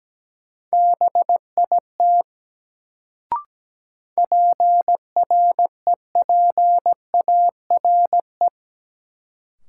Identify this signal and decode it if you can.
Morse code, heard as a keyed tone, about 17 wpm, 700 Hz. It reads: BIT PREPARE
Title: Sentences from Top 900 Words - New Word Drill - 17wpm